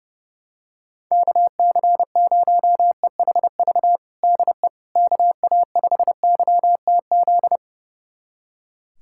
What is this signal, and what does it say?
Morse code, heard as a keyed tone, about 32 wpm, 700 Hz. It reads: KC0EHV DE KA5YTZ
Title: USA Callsign DE Callsign - 6 Characters - 30wpm